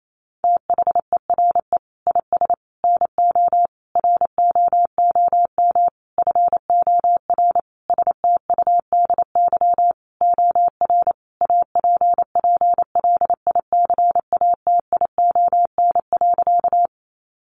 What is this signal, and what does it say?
Morse code, heard as a keyed tone, about 28 wpm, 700 Hz. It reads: THERE IS NO ROOM FOR STUDY OR APPLICATION.
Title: Sentences from Top 1200 Words 28wpm